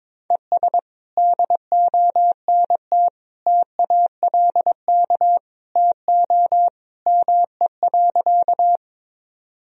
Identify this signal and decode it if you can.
Morse code, heard as a keyed tone, about 22 wpm, 700 Hz. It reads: ES DONT TALK TO ME.
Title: Sentences from Top 400 Words and Abbreviations 22wpm